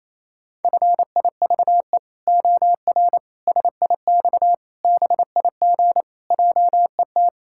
Morse code, heard as a keyed tone, about 28 wpm, 700 Hz. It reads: FIVE OR SIX BIG JET
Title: Pangram Challenge v2 - 28wpm